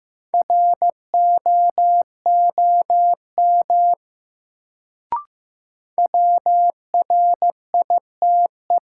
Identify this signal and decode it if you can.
Morse code, heard as a keyed tone, about 15 wpm, 700 Hz. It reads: ROOM WRITE
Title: Sentences from Top 300 Words - New Word Drill - 15wpm